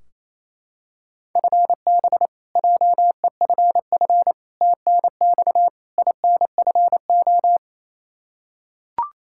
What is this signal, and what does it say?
Morse code, heard as a keyed tone, about 28 wpm, 700 Hz. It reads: FB JEFF TNX INFO
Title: Practice QSO - Exchange 3 - More Info - 28wpm